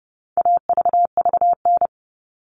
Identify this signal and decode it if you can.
Morse code, heard as a keyed tone, about 30 wpm, 700 Hz. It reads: AVVN
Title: All Cut Numbers - 4-Digit 30wpm